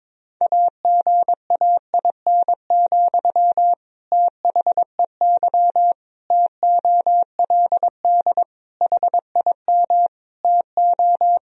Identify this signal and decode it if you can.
Morse code, heard as a keyed tone, about 22 wpm, 700 Hz. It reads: AGAIN, THEY TOLD HIM TO
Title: Sentences from Top 1000 Words - Review Part 3 - 22wpm